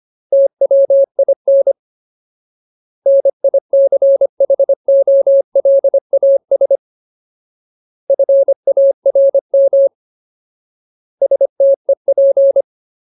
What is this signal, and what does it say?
Morse code, heard as a keyed tone, about 25 wpm, 550 Hz. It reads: TWIN NICHOLAS FARM STEP
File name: Ultimate Word List / 25WPM Ultimate